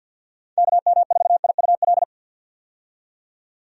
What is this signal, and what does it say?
Morse code, heard as a keyed tone, about 51 wpm, 700 Hz. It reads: KM4IUL
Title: USA Call Signs - 6 Characters - 50wpm